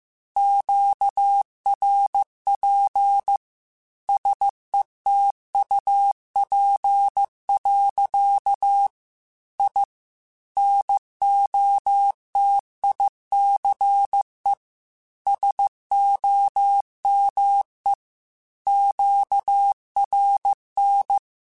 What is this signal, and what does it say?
Morse code, heard as a keyed tone, about 15 wpm, 800 Hz. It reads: QRP SETUP. I NOTICE SOME QRN
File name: QSOs / QSOs-mp3s / WD0BGP 1@10